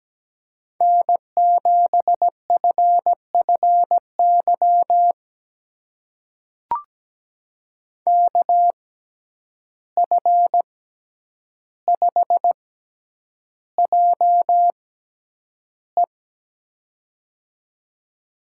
Morse code, heard as a keyed tone, about 17 wpm, spaced as for 9 wpm, 700 Hz. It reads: N7FFY KF5JE
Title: USA Call Signs - 5 Characters - 6x Character Spacing - 17wpm